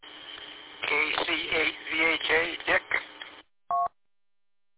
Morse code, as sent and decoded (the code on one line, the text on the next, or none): none